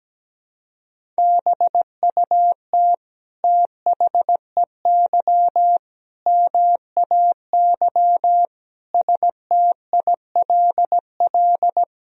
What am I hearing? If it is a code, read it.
Morse code, heard as a keyed tone, about 17 wpm, 700 Hz. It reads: BUT THEY MAY STILL
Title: Sentences from Top 300 Words and Abbreviations 17wpm